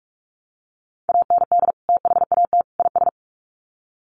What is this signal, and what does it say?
Morse code, heard as a keyed tone, about 44 wpm, 700 Hz. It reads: AND THAT IS
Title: Sets of 3 Words 45wpm